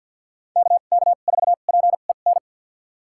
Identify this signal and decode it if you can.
Morse code, heard as a keyed tone, about 50 wpm, spaced as for 36 wpm, 700 Hz. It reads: KK4PEN
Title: USA Call Signs - 2x Character Spacing - 50wpm